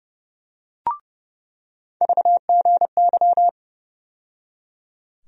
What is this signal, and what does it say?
Morse code, heard as a keyed tone, about 30 wpm, 700 Hz. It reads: VGY
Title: USA Call Sign Postfixes - 3 Letters 30wpm